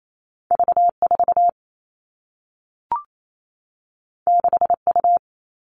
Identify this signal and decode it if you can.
Morse code, heard as a keyed tone, about 28 wpm, 700 Hz. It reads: V4 6U
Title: All Cut Numbers - 2-Digit 28wpm